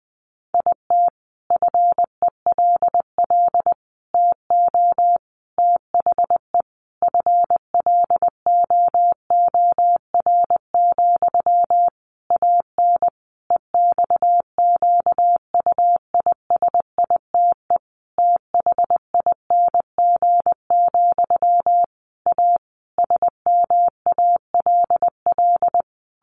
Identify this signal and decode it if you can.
Morse code, heard as a keyed tone, about 20 wpm, 700 Hz. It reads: IT FELL TO THE FLOOR, AN EXQUISITE THING, A SMALL
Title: A Sound Of Thunder 20wpm